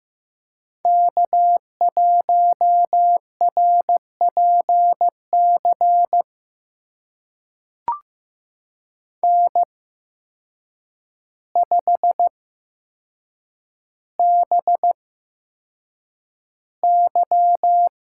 Morse code, heard as a keyed tone, about 15 wpm, spaced as for 7 wpm, 700 Hz. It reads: K1RPC N5BY
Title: USA Call Signs - 5 Characters - 8x Character Spacing - 15wpm